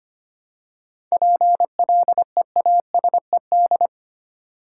Morse code, heard as a keyed tone, about 25 wpm, 700 Hz. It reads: PLEASED
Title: Seven Letter Words - 25wpm